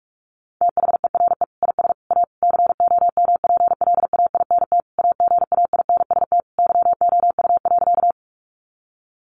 Morse code, heard as a keyed tone, about 44 wpm, 700 Hz. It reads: THERE IS A COMPLAINT AGAINST YOU.
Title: Sentences from Top 1400 Words 45wpm